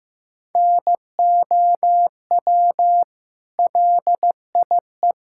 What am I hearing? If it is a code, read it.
Morse code, heard as a keyed tone, about 15 wpm, 700 Hz. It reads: NOW LIE
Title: Sentences from Top 1600 Words 15wpm